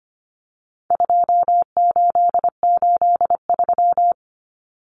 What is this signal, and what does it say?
Morse code, heard as a keyed tone, about 25 wpm, 700 Hz. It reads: 2883
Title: Common Cut Numbers - 4-Digit 25wpm